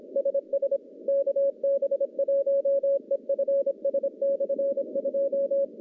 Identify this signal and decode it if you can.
Morse code, heard as a keyed tone, about 26 wpm, 550 Hz. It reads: SS KB1EFS/2
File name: contests / CW-SS-k5tr- -1100